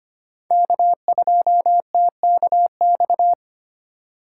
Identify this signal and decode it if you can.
Morse code, heard as a keyed tone, about 25 wpm, 700 Hz. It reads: K2TKX
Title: USA Call Signs - 2x Character Spacing - 25wpm